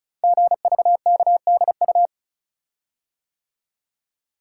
Morse code, heard as a keyed tone, about 36 wpm, 700 Hz. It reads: GVKDU
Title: Code Groups - 5 Letter:Numbers - 35wpm